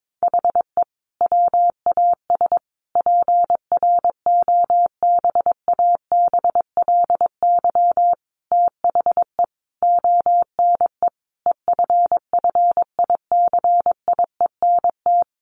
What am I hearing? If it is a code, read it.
Morse code, heard as a keyed tone, about 22 wpm, 700 Hz. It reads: HE WAS PROBABLY THE ONE EFFICIENT
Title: Sentences from Top 1300 Words 22wpm